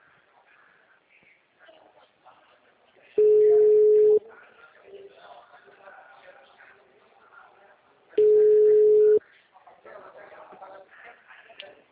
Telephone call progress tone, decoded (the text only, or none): ringback tone